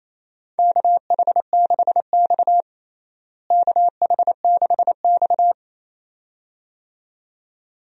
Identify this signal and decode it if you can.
Morse code, heard as a keyed tone, about 27 wpm, 700 Hz. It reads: KH6X KH6X
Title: DXCC-100 Call Signs - 4 Characters - Repeated 2x - 28wpm